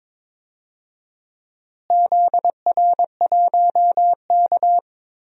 Morse code, heard as a keyed tone, about 22 wpm, 700 Hz. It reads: ZR1K
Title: DXCC-100 Call Signs - 4 Characters - Repeated 2x - 22wpm